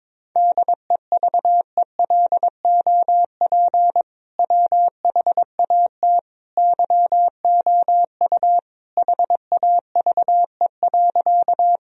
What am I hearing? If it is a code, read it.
Morse code, heard as a keyed tone, about 21 wpm, 700 Hz. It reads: DEVELOP WHAT YOU HAVE.
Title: Sentences from Top 1000 Words - Review Part 2 - 22wpm